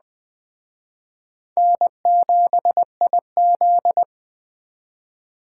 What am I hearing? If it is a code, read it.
Morse code, heard as a keyed tone, about 20 wpm, 700 Hz. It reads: N7IZ
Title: Speed Racing - USA 1x2 Call Signs 17wpm